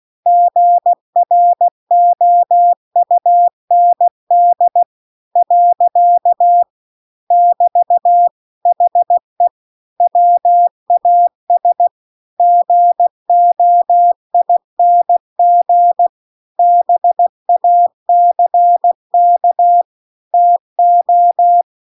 Morse code, heard as a keyed tone, about 16 wpm, 700 Hz. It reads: GROUND . <BT> HE WAS GOING BACK TO